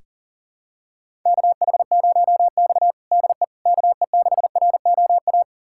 Morse code, heard as a keyed tone, about 40 wpm, 700 Hz. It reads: KH0X DE KE6ROA
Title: USA Callsign DE Callsign 40wpm